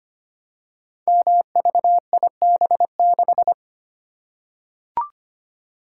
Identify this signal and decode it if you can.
Morse code, heard as a keyed tone, about 25 wpm, 700 Hz. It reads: MVIB6
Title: Code Groups - 5 Letter:Numbers - 25wpm